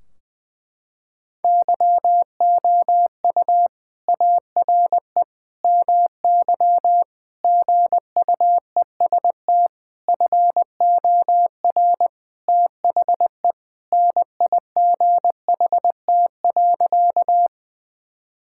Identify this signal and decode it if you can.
Morse code, heard as a keyed tone, about 20 wpm, 700 Hz. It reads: YOU ARE MY GUEST FOR THE NIGHT.
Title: Sentences from Top 1600 Words 20wpm